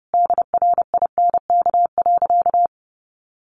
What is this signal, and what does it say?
Morse code, heard as a keyed tone, about 30 wpm, 700 Hz. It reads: DRINK.
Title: Sentences from Top 1500 Words 30wpm